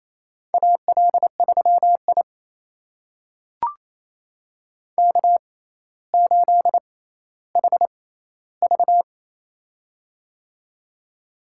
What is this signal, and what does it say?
Morse code, heard as a keyed tone, about 27 wpm, spaced as for 14 wpm, 700 Hz. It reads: AL3I K8HV
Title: USA Call Signs - 4 Characters - 6x Character Spacing - 28wpm